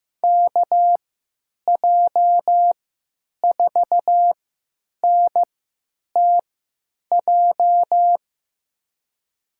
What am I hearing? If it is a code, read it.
Morse code, heard as a keyed tone, about 15 wpm, spaced as for 9 wpm, 700 Hz. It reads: KJ4NTJ
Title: USA Call Signs - 6 Characters - 3x Character Spacing - 15wpm